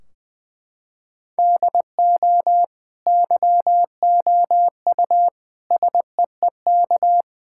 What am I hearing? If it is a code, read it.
Morse code, heard as a keyed tone, about 20 wpm, 700 Hz. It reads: DO YOU SEEK
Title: Sentences from Top 1600 Words 20wpm